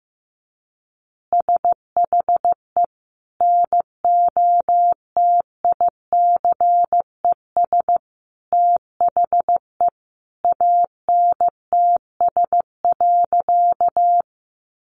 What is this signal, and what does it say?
Morse code, heard as a keyed tone, about 15 wpm, 700 Hz. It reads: SHE NOTICES THE ANTS.
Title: Easy Phrases 15wpm